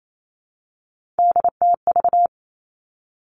Morse code, heard as a keyed tone, about 28 wpm, 700 Hz. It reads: DTV
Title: Three Challenging Letter-Number Combinations 28wpm